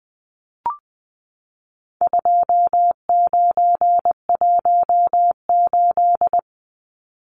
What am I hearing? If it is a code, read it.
Morse code, heard as a keyed tone, about 20 wpm, 700 Hz. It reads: 2918
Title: Numbers - 4-Digit 20wpm